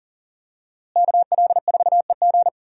Morse code, heard as a keyed tone, about 40 wpm, 700 Hz. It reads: KL4EG
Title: USA Call Signs - 5 Characters - Repeated 2x - 40wpm